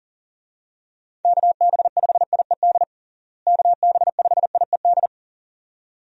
Morse code, heard as a keyed tone, about 40 wpm, 700 Hz. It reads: KB5IED KB5IED
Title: USA Call Signs - 6 Characters - Repeated 2x - 40wpm